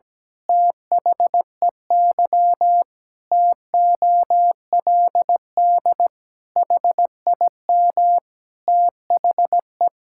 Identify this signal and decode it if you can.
Morse code, heard as a keyed tone, about 17 wpm, 700 Hz. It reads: THEY TOLD HIM THE
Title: Pangram Challenge v2 - 17wpm